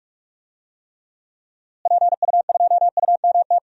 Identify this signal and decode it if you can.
Morse code, heard as a keyed tone, about 46 wpm, 700 Hz. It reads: PU2UMT
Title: DXCC-100 Call Signs - 6 Characters - Repeated 3x - 45wpm